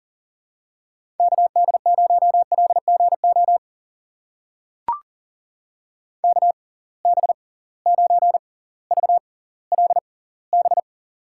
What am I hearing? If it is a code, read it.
Morse code, heard as a keyed tone, about 40 wpm, spaced as for 20 wpm, 700 Hz. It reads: KD0LGO KB9VLB
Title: USA Call Signs - 6 Characters - 6x Character Spacing - 40wpm